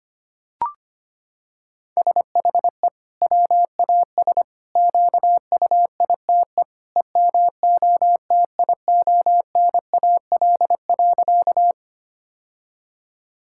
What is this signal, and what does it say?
Morse code, heard as a keyed tone, about 25 wpm, 700 Hz. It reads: SHE WAS QUITE EMOTIONAL.
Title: Sentences from Top 1000 Words - Review Part 5 - 25wpm